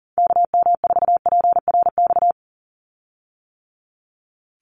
Morse code, heard as a keyed tone, about 40 wpm, 690 Hz. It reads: KM4PRX
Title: USA Call Signs 40wpm